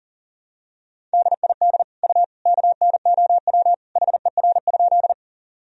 Morse code, heard as a keyed tone, about 40 wpm, 700 Hz. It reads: DID U KNOW HER?
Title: Sentences from Top 100 Words and Abbreviations 40wpm